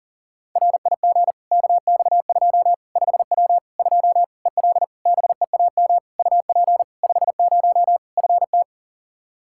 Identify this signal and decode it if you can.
Morse code, heard as a keyed tone, about 40 wpm, 700 Hz. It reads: RIG KX2 5W 2 EL BEAM UP 50 FT
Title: Practice QSO - Exchange 2 - Info - 40wpm